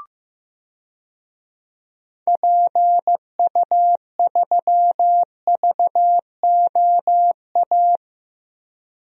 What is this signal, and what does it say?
Morse code, heard as a keyed tone, about 15 wpm, 700 Hz. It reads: PU3VOA
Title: DXCC-100 Call Signs - 6 Characters - Repeated 3x - 15wpm